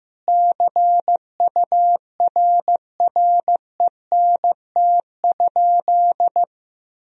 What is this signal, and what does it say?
Morse code, heard as a keyed tone, about 15 wpm, 700 Hz. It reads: CURRENT?
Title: Sentences from Top 1000 Words - Review Part 4 - 15wpm